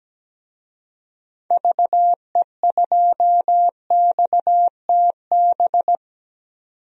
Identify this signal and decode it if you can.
Morse code, heard as a keyed tone, about 17 wpm, 700 Hz. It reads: VE2XTB
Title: DXCC-100 Call Signs - 6 Characters - 17wpm